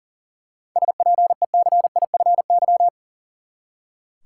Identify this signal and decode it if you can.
Morse code, heard as a keyed tone, about 40 wpm, 700 Hz. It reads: SPECIFY
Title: Seven Letter Words - 40wpm